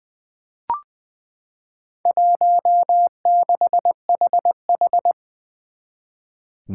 Morse code, heard as a keyed tone, about 20 wpm, 700 Hz. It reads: 16HH
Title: Four Challenging Letter-Number Combinations 20wpm